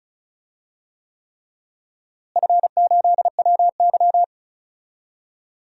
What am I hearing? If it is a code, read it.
Morse code, heard as a keyed tone, about 34 wpm, 700 Hz. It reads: F8WY
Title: DXCC-100 Call Signs - 4 Characters - Repeated 3x - 35wpm